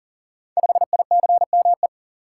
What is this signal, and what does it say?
Morse code, heard as a keyed tone, about 40 wpm, 700 Hz. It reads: 5ICME
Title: Code Groups - 5 Letter:Numbers - 40wpm